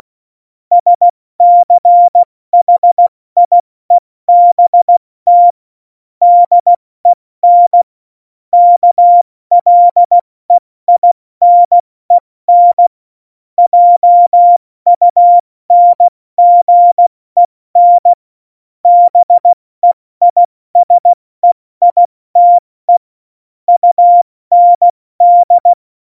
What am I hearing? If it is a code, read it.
Morse code, heard as a keyed tone, about 16 wpm, 700 Hz. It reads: SCHIEBT DEN KLEINEN JUNGEN BEISEITE UND
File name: Morsekurs / Witze / Witz 29. Januar 2019, 70 F80